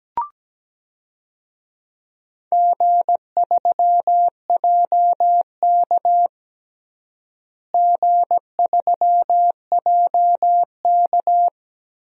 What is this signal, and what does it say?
Morse code, heard as a keyed tone, about 17 wpm, 700 Hz. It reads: G3JK G3JK
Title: DXCC-100 Call Signs - 4 Characters - Repeated 2x - 17wpm